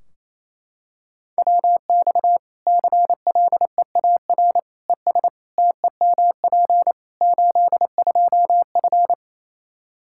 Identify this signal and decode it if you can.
Morse code, heard as a keyed tone, about 27 wpm, 700 Hz. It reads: WX CLEAR ES TEMP 82F
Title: Practice QSO - Exchange 2 - Info - 28wpm